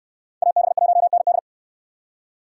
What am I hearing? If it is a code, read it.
Morse code, heard as a keyed tone, about 64 wpm, 700 Hz. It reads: AB1ND